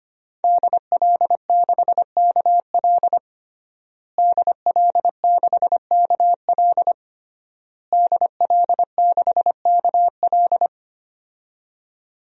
Morse code, heard as a keyed tone, about 25 wpm, 700 Hz. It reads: DL6KL DL6KL DL6KL
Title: DXCC-100 Call Signs - 5 Characters - Repeated 3x - 25wpm